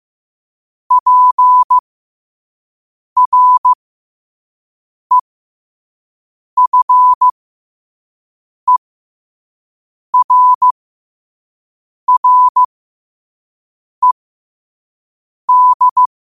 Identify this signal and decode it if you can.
Morse code, heard as a keyed tone, about 15 wpm, spaced as for 5 wpm, 1000 Hz. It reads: PREFERRED
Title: 5 WPM morse code quotes for Fri, 01 Mar 2024